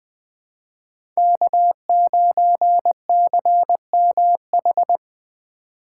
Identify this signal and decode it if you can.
Morse code, heard as a keyed tone, about 20 wpm, 700 Hz. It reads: K9CMH
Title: USA Call Signs - 5 Characters - 3x Character Spacing - 20wpm